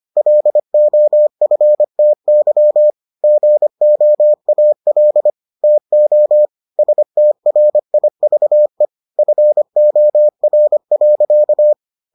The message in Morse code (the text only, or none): LOFTY GOAL TO STRIVE FOR.